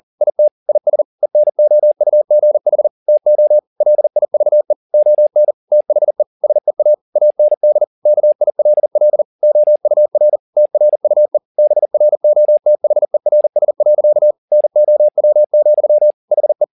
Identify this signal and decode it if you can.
Morse code, heard as a keyed tone, about 40 wpm, 600 Hz. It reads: IT IS ENOUGH TO LIVE ON THE SEA AND KILL OUR TRUE BROTHERS. NOW, HE
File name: the old man and the sea 2 40wpm